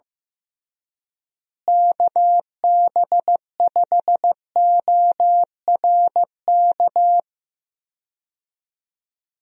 Morse code, heard as a keyed tone, about 15 wpm, 700 Hz. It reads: KB5ORK
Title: USA Call Signs - 6 Characters - Repeated 2x - 15wpm